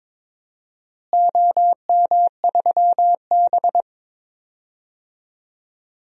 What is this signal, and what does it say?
Morse code, heard as a keyed tone, about 22 wpm, 700 Hz. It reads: OM3B
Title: DXCC-100 Call Signs - 4 Characters - Repeated 3x - 22wpm